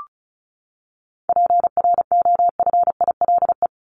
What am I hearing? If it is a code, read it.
Morse code, heard as a keyed tone, about 34 wpm, 700 Hz. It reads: PROFILE
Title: Top 1501-1750 Words 35wpm